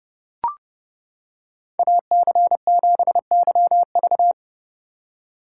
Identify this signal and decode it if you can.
Morse code, heard as a keyed tone, about 30 wpm, 700 Hz. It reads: AC7YV